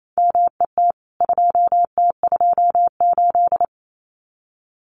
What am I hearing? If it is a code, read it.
Morse code, heard as a keyed tone, about 28 wpm, 700 Hz. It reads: MET 2T28
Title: CWT Exchange 28wpm